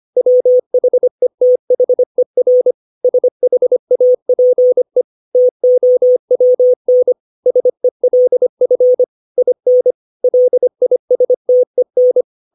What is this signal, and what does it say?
Morse code, heard as a keyed tone, about 25 wpm, 500 Hz. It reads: WHETHER SHAPE TOWN SELF IN LISTEN
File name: 632 words / 632 Words 25WPM